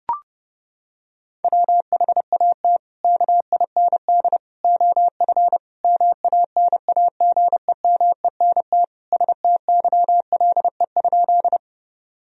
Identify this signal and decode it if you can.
Morse code, heard as a keyed tone, about 30 wpm, 700 Hz. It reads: WHAT KIND OF MANAGEMENT STYLE?
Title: Sentences from Top 1000 Words - Review Part 4 - 30wpm